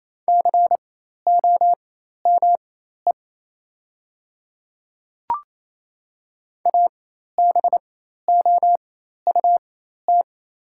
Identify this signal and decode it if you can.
Morse code, heard as a keyed tone, about 28 wpm, spaced as for 13 wpm, 700 Hz. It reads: COME ABOUT
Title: 4x Character Spacing - Top 100 Words - 28wpm